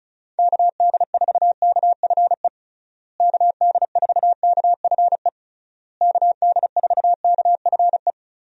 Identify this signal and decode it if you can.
Morse code, heard as a keyed tone, about 34 wpm, 700 Hz. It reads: KD4KFE KD4KFE KD4KFE
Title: USA Call Signs - 6 Characters - Repeated 3x - 35wpm